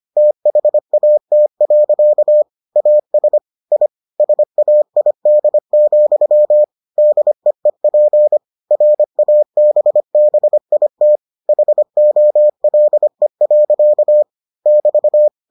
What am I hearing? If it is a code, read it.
Morse code, heard as a keyed tone, about 25 wpm, 600 Hz. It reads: THAT. AS I SAID, DEEP RABBIT HOLE. <BT>